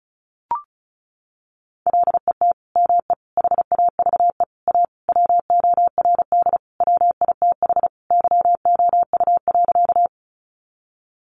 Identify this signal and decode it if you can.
Morse code, heard as a keyed tone, about 34 wpm, 700 Hz. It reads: LET ME HAVE A WORD WITH YOU.
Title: Sentences from Top 300 Words 35wpm